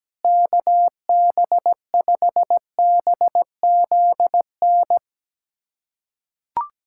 Morse code, heard as a keyed tone, about 17 wpm, 700 Hz. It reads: KB5BZN
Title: USA Callsign DE Callsign - 6 Characters - 17wpm